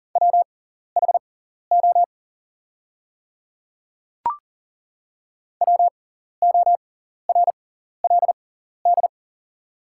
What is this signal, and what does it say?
Morse code, heard as a keyed tone, about 40 wpm, spaced as for 14 wpm, 690 Hz. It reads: WHO WORLD
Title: 6x Character Spacing - Top 200 Words - 40wpm